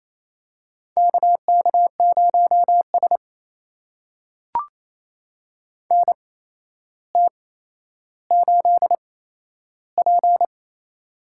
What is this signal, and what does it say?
Morse code, heard as a keyed tone, about 28 wpm, spaced as for 12 wpm, 700 Hz. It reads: KK0S NT8P
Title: USA Call Signs - 4 Characters - 8x Character Spacing - 28wpm